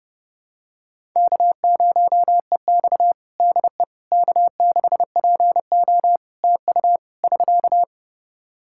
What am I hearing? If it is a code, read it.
Morse code, heard as a keyed tone, about 30 wpm, 700 Hz. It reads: K0EX DE K6PO TU <SK>
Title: Practice QSO - Exchange 4 - Ending - 30wpm